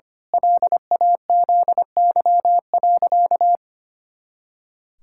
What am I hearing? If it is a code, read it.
Morse code, heard as a keyed tone, about 25 wpm, 700 Hz. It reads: LAZY.
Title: Pangram Challenge v2 - 25wpm